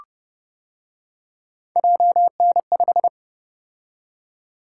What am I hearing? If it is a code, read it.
Morse code, heard as a keyed tone, about 29 wpm, 700 Hz. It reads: JN5